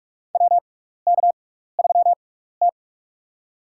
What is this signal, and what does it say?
Morse code, heard as a keyed tone, about 46 wpm, spaced as for 15 wpm, 700 Hz. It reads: WK3T
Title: USA Call Signs - 4 Characters - 6x Character Spacing - 45wpm